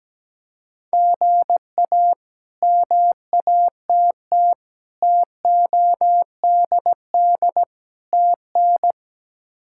Morse code, heard as a keyed tone, about 17 wpm, 700 Hz. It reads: GA MATT TODD TN
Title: K1USN SST QSO Practice - 17wpm